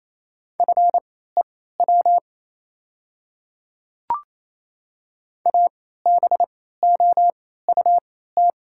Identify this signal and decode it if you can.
Morse code, heard as a keyed tone, about 27 wpm, spaced as for 16 wpm, 700 Hz. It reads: FEW ABOUT
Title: 3x Character Spacing - Top 300 Words - 28wpm